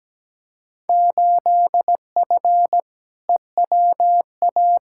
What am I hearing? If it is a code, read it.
Morse code, heard as a keyed tone, about 17 wpm, 700 Hz. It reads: 8F EWA
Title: Field Day Exchanges Encore - 17wpm